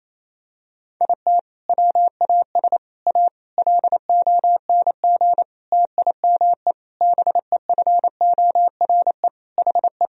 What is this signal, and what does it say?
Morse code, heard as a keyed tone, about 28 wpm, 700 Hz. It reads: IT WAS A LONG TIME BEFORE HE
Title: Sentences from Top 1000 Words - Review Part 2 - 28wpm